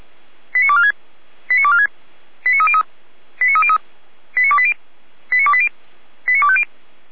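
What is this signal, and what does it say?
Five-tone selective call (ZVEI), heard as quick runs of five tones, tones of about 70 ms. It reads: 89246 89246 89303 89303 89280 89280 89250
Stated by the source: Five-tone ZVEI selective call received on an SDR.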